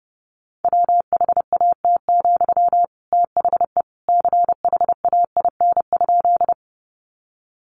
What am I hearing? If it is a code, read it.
Morse code, heard as a keyed tone, about 30 wpm, 700 Hz. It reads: WHAT, THE CHAIN?
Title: Sentences from Top 1500 Words 30wpm